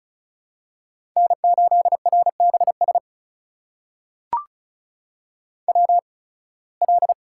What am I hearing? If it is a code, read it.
Morse code, heard as a keyed tone, about 34 wpm, spaced as for 23 wpm, 700 Hz. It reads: N8RBS WL
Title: USA Call Signs - 5 Characters - 8x Character Spacing - 35wpm